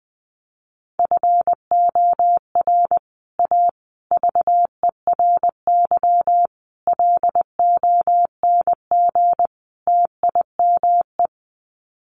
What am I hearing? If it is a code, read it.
Morse code, heard as a keyed tone, about 20 wpm, 700 Hz. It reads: FOR A VERY LONG TIME
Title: Sets of 5 Words 20wpm